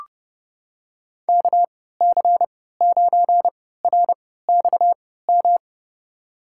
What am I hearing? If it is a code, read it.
Morse code, heard as a keyed tone, about 32 wpm, spaced as for 17 wpm, 700 Hz. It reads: KC9RXM
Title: USA Call Signs - 6 Characters - 3x Character Spacing - 30wpm